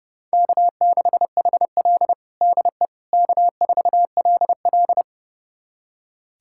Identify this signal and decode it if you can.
Morse code, heard as a keyed tone, about 32 wpm, 700 Hz. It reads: K6HL DE K4LL
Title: USA Callsign DE Callsign - 4 Characters - 30wpm